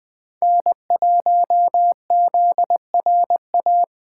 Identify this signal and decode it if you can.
Morse code, heard as a keyed tone, about 20 wpm, 700 Hz. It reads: N1ZRA